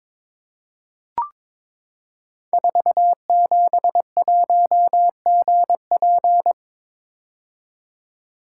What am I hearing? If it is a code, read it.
Morse code, heard as a keyed tone, about 22 wpm, 700 Hz. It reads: 471GP
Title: Code Groups - 5 Letter:Numbers - 22wpm